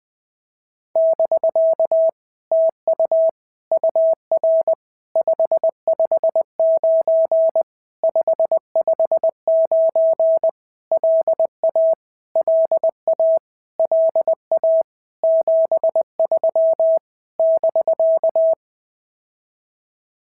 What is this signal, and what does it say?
Morse code, heard as a keyed tone, about 20 wpm, 650 Hz. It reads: <BK> TU UR 559 559 LA LA LA 73 <BK>
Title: POTA QSO Practice - 20wpm